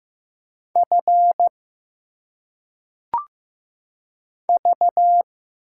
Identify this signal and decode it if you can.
Morse code, heard as a keyed tone, about 15 wpm, 700 Hz. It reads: F V
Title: Single Letters - Phonetic Pronunciation 15wpm